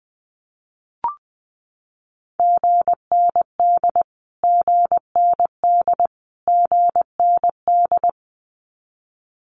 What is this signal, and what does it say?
Morse code, heard as a keyed tone, about 20 wpm, 700 Hz. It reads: GND GND GND
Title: Abbreviations Repeated 3x - 20wpm